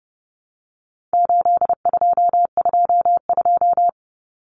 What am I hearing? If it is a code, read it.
Morse code, heard as a keyed tone, about 30 wpm, 700 Hz. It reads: 8222
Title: Common Cut Numbers - 4-Digit 30wpm